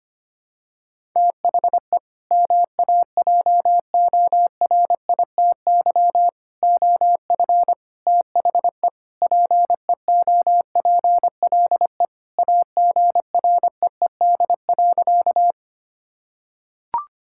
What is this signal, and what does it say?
Morse code, heard as a keyed tone, about 25 wpm, 700 Hz. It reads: THE MAJORITY OF THE PEOPLE AGREED.
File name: practice / 25wpm / Sentences with 4-Word Phrases - 25wpm